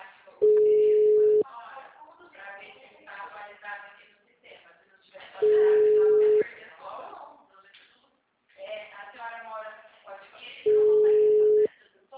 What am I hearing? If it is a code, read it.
Ringback tone (Europe, 425 Hz), with speech.